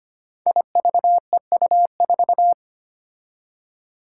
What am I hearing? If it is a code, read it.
Morse code, heard as a keyed tone, about 25 wpm, 700 Hz. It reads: IVEU4
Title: Five Challenging Letter-Number Combinations 25wpm